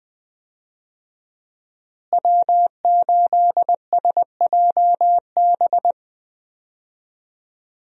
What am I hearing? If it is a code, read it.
Morse code, heard as a keyed tone, about 20 wpm, 700 Hz. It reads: W8SJB
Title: USA Call Signs - 5 Characters - Repeated 3x - 20wpm